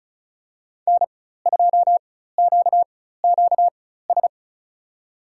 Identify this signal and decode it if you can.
Morse code, heard as a keyed tone, about 36 wpm, spaced as for 16 wpm, 700 Hz. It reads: N2QQS
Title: USA Call Signs - 5 Characters - 4x Character Spacing - 35wpm